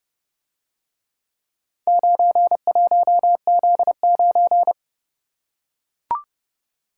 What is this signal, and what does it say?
Morse code, heard as a keyed tone, about 30 wpm, 700 Hz. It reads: 91Z9